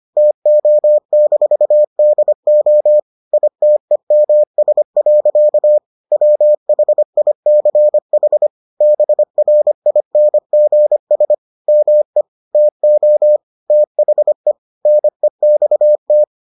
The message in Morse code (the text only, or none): TO-DO ITEMS. WHICH BRINGS ME TO THE NEXT